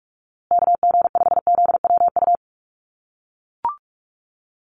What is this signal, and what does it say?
Morse code, heard as a keyed tone, about 44 wpm, 700 Hz. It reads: KG5ZWU
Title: USA Call Signs Repeated 3x 45wpm